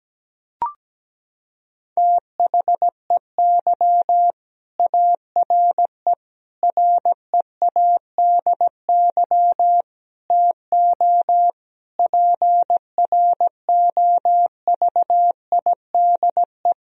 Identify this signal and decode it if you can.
Morse code, heard as a keyed tone, about 17 wpm, 700 Hz. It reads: THEY ARE READY TO PROVIDE
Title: Sentences from Top 1000 Words - Review Part 3 - 17wpm